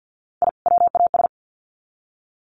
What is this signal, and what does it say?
Morse code, heard as a keyed tone, about 50 wpm, 700 Hz. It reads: I WAS
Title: Sets of 2 Words 50wpm